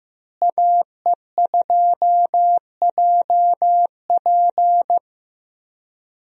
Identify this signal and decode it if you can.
Morse code, heard as a keyed tone, about 15 wpm, 700 Hz. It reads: AE2JP
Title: USA Call Signs - 5 Characters - Repeated 3x - 15wpm